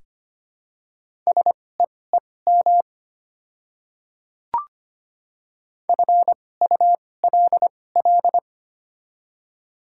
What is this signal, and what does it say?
Morse code, heard as a keyed tone, about 25 wpm, spaced as for 18 wpm, 700 Hz. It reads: SEEM FULL
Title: 2x Character Spacing - Top 500 Words - 25wpm